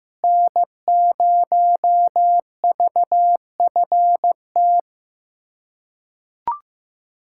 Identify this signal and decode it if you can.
Morse code, heard as a keyed tone, about 16 wpm, 700 Hz. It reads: N0VFT
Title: USA Callsign DE Callsign 15wpm